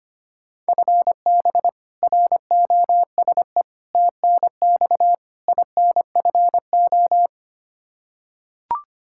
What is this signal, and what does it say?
Morse code, heard as a keyed tone, about 25 wpm, 700 Hz. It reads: FB ROSE TNX INFO
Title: Practice QSO - Exchange 3 - More Info - 25wpm